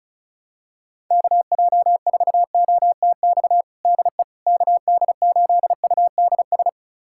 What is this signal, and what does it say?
Morse code, heard as a keyed tone, about 36 wpm, 700 Hz. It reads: KJ4OTX DE KD8UDS
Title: USA Callsign DE Callsign - 6 Characters - 35wpm